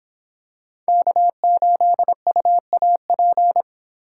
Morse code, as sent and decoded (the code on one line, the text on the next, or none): -.- ---.. ..- .- .--.
K8UAP